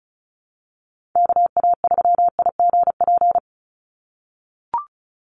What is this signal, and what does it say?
Morse code, heard as a keyed tone, about 34 wpm, 700 Hz. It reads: KA3IGP
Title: USA Call Signs 35wpm